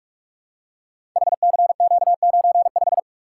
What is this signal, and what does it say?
Morse code, heard as a keyed tone, about 46 wpm, 700 Hz. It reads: HCQ95
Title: Code Groups - 5 Letter:Numbers - 45wpm